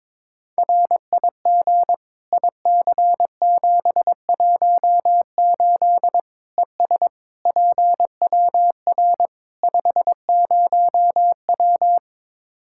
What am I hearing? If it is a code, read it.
Morse code, heard as a keyed tone, about 22 wpm, 700 Hz. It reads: RIG IC718 ES PWR 50W
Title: Practice QSO - Exchange 2 - Info - 22wpm